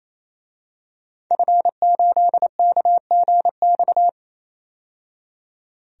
Morse code, heard as a keyed tone, about 28 wpm, 700 Hz. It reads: F8KGX
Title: DXCC-100 Call Signs - 5 Characters - 28wpm